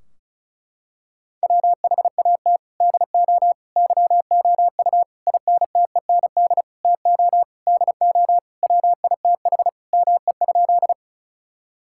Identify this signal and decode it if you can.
Morse code, heard as a keyed tone, about 36 wpm, 700 Hz. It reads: WHAT DO YOU INTEND TO DO WITH ME?
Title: Sentences from Top 1600 Words 35wpm